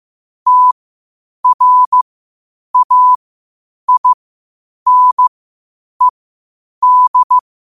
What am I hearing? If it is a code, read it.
Morse code, heard as a keyed tone, about 15 wpm, spaced as for 8 wpm, 1000 Hz. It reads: TRAINED
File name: qotdfiles / 8wpm